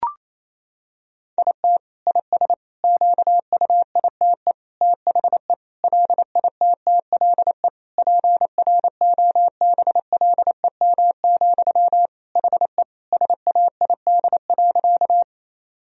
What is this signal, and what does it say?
Morse code, heard as a keyed tone, about 28 wpm, 700 Hz. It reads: IT IS QUITE THE LITTLE PROBLEM, HE SAID.
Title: Sentences from Top 1000 Words - Review Part 2 - 28wpm